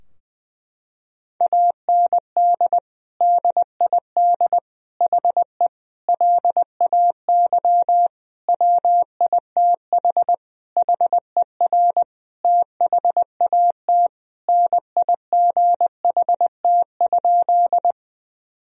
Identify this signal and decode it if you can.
Morse code, heard as a keyed tone, about 20 wpm, 700 Hz. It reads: AND DID HE LAY WITH HER THAT NIGHT?
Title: Sentences from Top 1000 Words - Review Part 2 - 20wpm